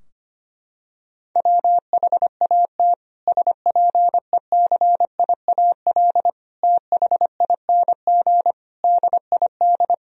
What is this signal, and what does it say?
Morse code, heard as a keyed tone, about 25 wpm, 700 Hz. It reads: WHAT SPECIAL THING DID
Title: Sentences from Top 1000 Words - Review Part 3 - 25wpm